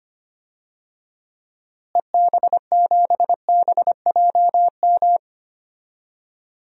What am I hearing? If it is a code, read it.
Morse code, heard as a keyed tone, about 25 wpm, 700 Hz. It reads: EB7BJM
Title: DXCC-100 Call Signs - 6 Characters - Repeated 3x - 25wpm